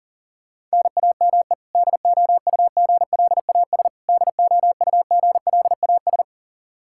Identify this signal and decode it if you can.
Morse code, heard as a keyed tone, about 40 wpm, 700 Hz. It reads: NAME DOUGLAS DOUGLAS
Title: Practice QSO - Exchange 1 - Beginning - 40wpm